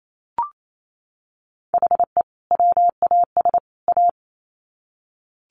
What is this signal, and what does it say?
Morse code, heard as a keyed tone, about 28 wpm, 700 Hz. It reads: HE WAS A
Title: Sets of 3 Words 28wpm